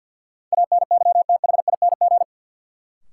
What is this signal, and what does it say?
Morse code, heard as a keyed tone, about 46 wpm, 700 Hz. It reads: ANYTHING
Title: Sentences from Top 1000 Words - New Word Drill - 50wpm